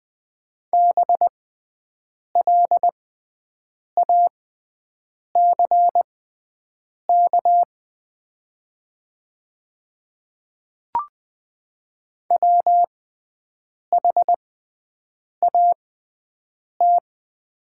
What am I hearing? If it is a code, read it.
Morse code, heard as a keyed tone, about 20 wpm, spaced as for 7 wpm, 700 Hz. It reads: BLACK WHAT
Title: 6x Character Spacing - Top 500 Words - 20wpm